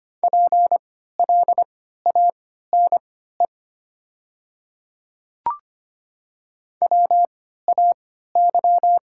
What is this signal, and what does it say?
Morse code, heard as a keyed tone, about 25 wpm, spaced as for 14 wpm, 700 Hz. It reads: PLANE WAY
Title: 3x Character Spacing - Top 500 Words - 25wpm